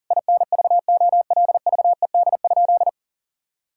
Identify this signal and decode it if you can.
Morse code, heard as a keyed tone, about 40 wpm, 700 Hz. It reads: INVOLVED?
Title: Sentences from Top 1000 Words - Review Part 5 - 40wpm